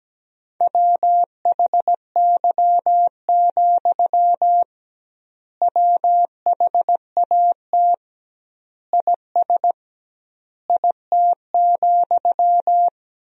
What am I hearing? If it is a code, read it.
Morse code, heard as a keyed tone, about 17 wpm, 700 Hz. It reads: WHY, WHAT IS IT,
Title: 2x Word-Spacing - Sentences from Top 200 Words - 17wpm